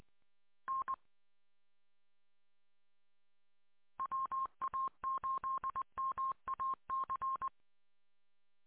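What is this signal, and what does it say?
Morse code, heard as a keyed tone, about 20 wpm, 1070 Hz. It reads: N WA8MAC
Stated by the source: Morse audio recorded from a 147.200 MHz repeater archive.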